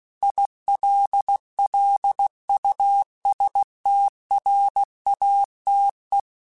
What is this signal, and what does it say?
Morse code, heard as a keyed tone, about 16 wpm, 800 Hz. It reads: ILLUSTRATE
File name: Technical Subjects / Technical Subjects-mp3s / AND Gate @15